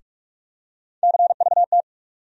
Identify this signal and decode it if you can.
Morse code, heard as a keyed tone, about 46 wpm, 700 Hz. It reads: CVT
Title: USA Call Sign Postfixes - 3 Letters 45wpm